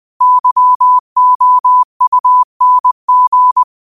Morse code, heard as a keyed tone, about 20 wpm, 1000 Hz. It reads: YOUNG